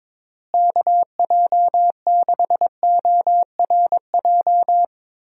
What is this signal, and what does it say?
Morse code, heard as a keyed tone, about 22 wpm, 700 Hz. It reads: KJ6ORJ
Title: USA Call Signs - 6 Characters - Repeated 2x - 22wpm